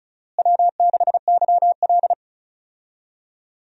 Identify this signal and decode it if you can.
Morse code, heard as a keyed tone, about 36 wpm, 700 Hz. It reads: W6YL
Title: USA Call Signs - 4 Characters - Repeated 3x - 35wpm